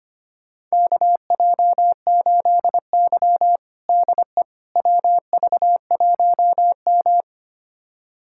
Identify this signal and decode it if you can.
Morse code, heard as a keyed tone, about 25 wpm, 700 Hz. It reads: KJ8Y DE WV1M
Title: USA Callsign DE Callsign - 4 Characters - 25wpm